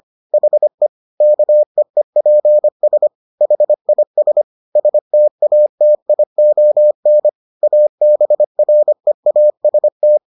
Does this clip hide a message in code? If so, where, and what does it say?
Yes: Morse code from 0.3 to 10.2 s, heard as a keyed tone, about 25 wpm, 600 Hz. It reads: HE KEEPS HIS STATION ABREAST